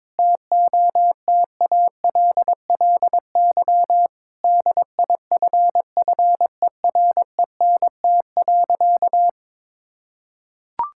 Morse code, heard as a keyed tone, about 22 wpm, 700 Hz. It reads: TOTALLY DIFFERENT.
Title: Sentences from Top 1600 Words 22wpm